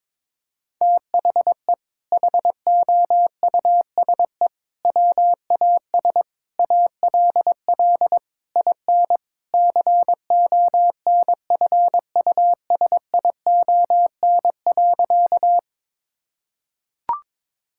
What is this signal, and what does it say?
Morse code, heard as a keyed tone, about 21 wpm, 700 Hz. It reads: THE HOUSE WAS ALL IN CONFUSION.
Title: Sentences from Top 1600 Words 22wpm